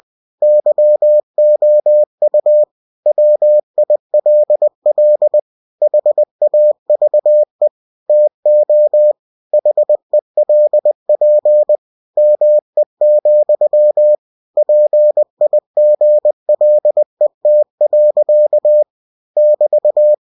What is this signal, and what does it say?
Morse code, heard as a keyed tone, about 20 wpm, 600 Hz. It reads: YOU WILL HAVE TO HELP ME, PIGLET. <BT>
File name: cw-books / winnie-the-pooh / winnie-20wpm-page-0034